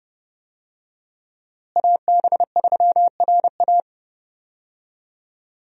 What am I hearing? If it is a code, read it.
Morse code, heard as a keyed tone, about 31 wpm, 700 Hz. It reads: AB3RA
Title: USA Call Signs - 5 Characters - Repeated 3x - 30wpm